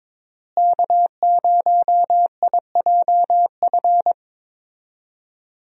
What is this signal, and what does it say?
Morse code, heard as a keyed tone, about 22 wpm, 700 Hz. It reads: K0IJF
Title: USA Call Signs - 5 Characters - Repeated 3x - 22wpm